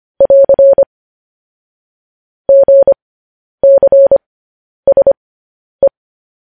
Morse code, heard as a keyed tone, about 25 wpm, spaced as for 10 wpm, 550 Hz. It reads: <AR> GCSE
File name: podcast / News headlines-1x-25x10